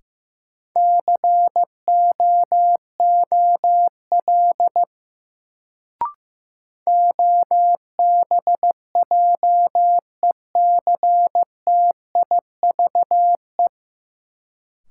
Morse code, heard as a keyed tone, about 15 wpm, 700 Hz. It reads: COOL OBJECTIVE
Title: Sentences from Top 1200 Words - New Word Drill - 15wpm